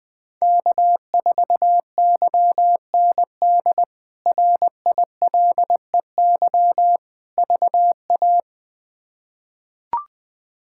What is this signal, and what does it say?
Morse code, heard as a keyed tone, about 20 wpm, 700 Hz. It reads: K4YND RILEY VA
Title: K1USN SST QSO Practice - 20wpm